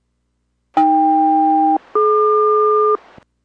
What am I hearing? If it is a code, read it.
Aircraft SELCAL, heard as two pairs of tones. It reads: AK-DQ